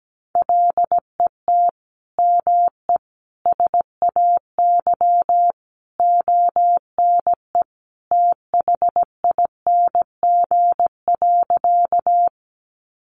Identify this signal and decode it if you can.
Morse code, heard as a keyed tone, about 17 wpm, 700 Hz. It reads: LET ME SAY ONE THING.
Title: Sentences from Top 300 Words 17wpm